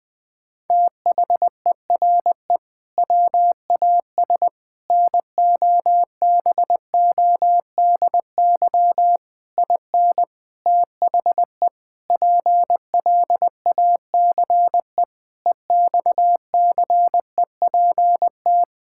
Morse code, heard as a keyed tone, about 20 wpm, 700 Hz. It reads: THERE WAS NOBODY IN THE PLACE EXCEPT
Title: Sentences from Top 1600 Words 20wpm